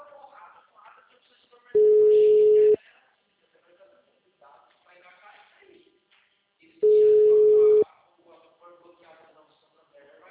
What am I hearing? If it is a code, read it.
Ringback tone (Europe, 425 Hz), with speech.
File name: out- -2011- - - .348